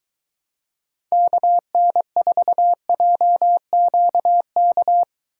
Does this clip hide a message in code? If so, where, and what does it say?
Yes: Morse code from 1.1 to 5.0 s, heard as a keyed tone, about 23 wpm, 700 Hz. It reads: KN4JQK